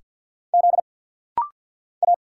Morse code, heard as a keyed tone, about 48 wpm, 690 Hz. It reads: Z A
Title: Single Letter-Number - Rapid-Fire 50wpm